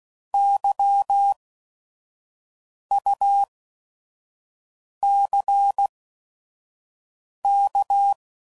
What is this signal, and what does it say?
Morse code, heard as a keyed tone, about 16 wpm, spaced as for 5 wpm, 800 Hz. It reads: YUCK
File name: Prefixes / By First Letter / Ys